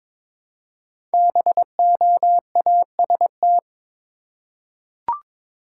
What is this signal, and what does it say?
Morse code, heard as a keyed tone, about 22 wpm, 700 Hz. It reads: BOAST
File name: 5 Letter Words - First 3 Letters Given - 22wpm